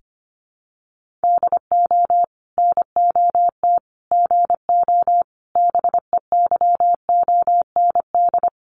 Morse code, heard as a keyed tone, about 25 wpm, 700 Hz. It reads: DO NOT GO BEYOND
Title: Sentences from Top 1500 Words 25wpm